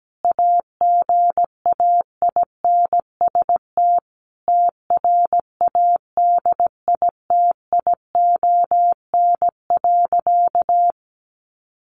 Morse code, heard as a keyed tone, about 17 wpm, 700 Hz. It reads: AGAINST TRADITION.
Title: Sentences from Top 1500 Words 17wpm